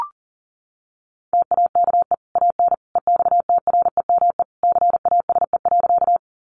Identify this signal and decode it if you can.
Morse code, heard as a keyed tone, about 40 wpm, 700 Hz. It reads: TAKE AN EXTREME CASE.
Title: Sentences from Top 1300 Words 40wpm